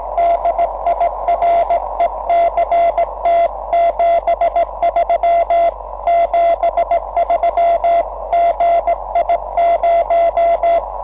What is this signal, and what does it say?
Morse code, heard as a keyed tone, about 18 wpm, 690 Hz, with noise. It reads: DIRECT 73 73 GI0
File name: 6M audio / vu2rm gi0otc cw cycle23